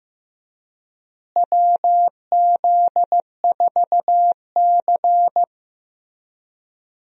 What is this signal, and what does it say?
Morse code, heard as a keyed tone, about 15 wpm, 700 Hz. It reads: WZ4C
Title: USA Call Signs - 4 Characters - Repeated 2x - 15wpm